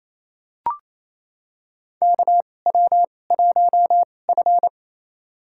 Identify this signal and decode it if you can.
Morse code, heard as a keyed tone, about 28 wpm, spaced as for 20 wpm, 700 Hz. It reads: KW1F
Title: USA Call Signs - 4 Characters - 2x Character Spacing - 28wpm